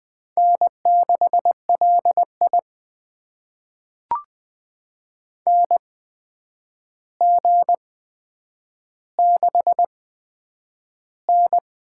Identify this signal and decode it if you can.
Morse code, heard as a keyed tone, about 20 wpm, spaced as for 9 wpm, 700 Hz. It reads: N6LI NG6N
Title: USA Call Signs - 4 Characters - 8x Character Spacing - 20wpm